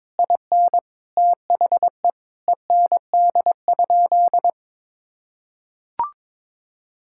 Morse code, heard as a keyed tone, about 22 wpm, 700 Hz. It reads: IN THE END?
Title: Sentences from Top 1000 Words - Review Part 5 - 22wpm